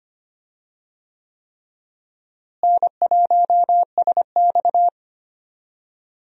Morse code, heard as a keyed tone, about 25 wpm, 700 Hz. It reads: N1SX
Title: USA Call Signs - 4 Characters - Repeated 3x - 25wpm